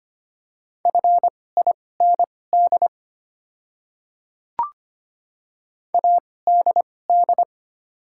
Morse code, heard as a keyed tone, about 25 wpm, spaced as for 18 wpm, 690 Hz. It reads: FIND ADD
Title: 2x Character Spacing - Top 200 Words - 25wpm